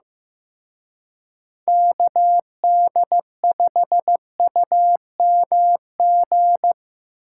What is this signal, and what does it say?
Morse code, heard as a keyed tone, about 15 wpm, 700 Hz. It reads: KD5UMG
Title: Speed Racing - USA Call Signs 15wpm